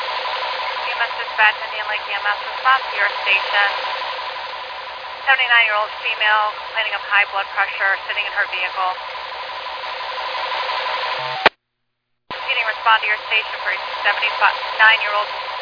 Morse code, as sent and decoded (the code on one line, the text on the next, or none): none